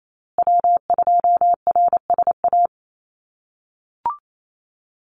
Morse code, heard as a keyed tone, about 28 wpm, 700 Hz. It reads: W2RSA
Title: USA Call Signs Repeated 3x 28wpm